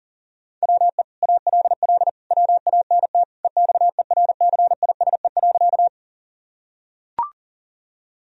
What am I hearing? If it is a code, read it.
Morse code, heard as a keyed tone, about 37 wpm, 700 Hz. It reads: WE ALL WANT EXERCISE.
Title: Sentences from Top 1000 Words - Review Part 4 - 40wpm